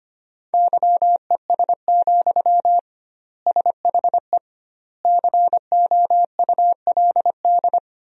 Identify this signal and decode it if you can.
Morse code, heard as a keyed tone, about 25 wpm, 700 Hz. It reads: YES, SHE COULD
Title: 2x Word-Spacing - Sentences from Top 500 Words - 25wpm